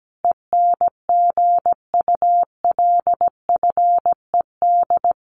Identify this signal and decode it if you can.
Morse code, heard as a keyed tone, about 17 wpm, 700 Hz. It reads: ENGULFED
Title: A Sound Of Thunder 17wpm